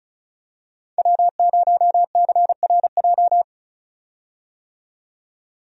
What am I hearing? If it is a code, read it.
Morse code, heard as a keyed tone, about 36 wpm, 700 Hz. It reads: W0CRJ
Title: USA Call Signs - 5 Characters - 35wpm